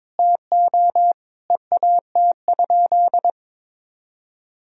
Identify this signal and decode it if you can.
Morse code, heard as a keyed tone, about 22 wpm, 700 Hz. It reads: TO EAT?
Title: Sentences from Top 300 Words and Abbreviations 22wpm